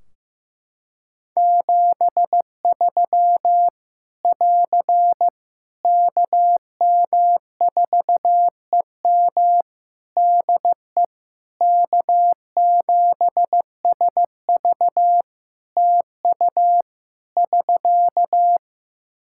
Morse code, heard as a keyed tone, about 15 wpm, 700 Hz. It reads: 73 <AR> KM4EM DE K7SV TU <SK>
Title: Practice QSO - Exchange 4 - Ending - 15wpm